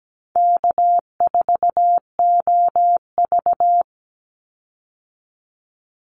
Morse code, heard as a keyed tone, about 17 wpm, 700 Hz. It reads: K4OV
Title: Call Signs Repeated 2x - 17wpm